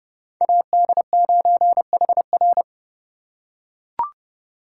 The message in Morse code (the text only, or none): AD9HR